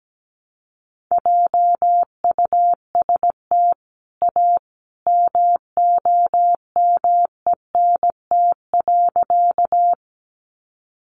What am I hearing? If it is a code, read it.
Morse code, heard as a keyed tone, about 17 wpm, 700 Hz. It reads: JUST A MOMENT.
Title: Sentences from Top 1000 Words 17wpm